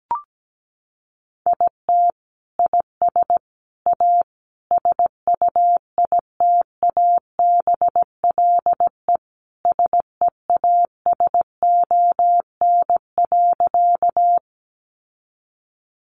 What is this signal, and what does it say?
Morse code, heard as a keyed tone, about 17 wpm, 700 Hz. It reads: IT IS A SUITABLE SEASON.
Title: Sentences from Top 1400 Words 17wpm